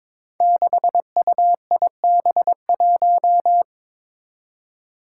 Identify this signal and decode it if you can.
Morse code, heard as a keyed tone, about 22 wpm, 700 Hz. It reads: 6UIB1
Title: Five Challenging Letter-Number Combinations 22wpm